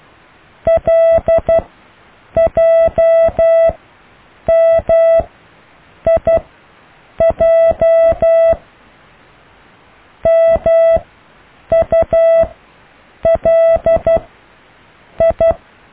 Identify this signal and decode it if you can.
Morse code, heard as a keyed tone, about 12 wpm, 660 Hz. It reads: L J M I J M U L I
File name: cw / IJLMU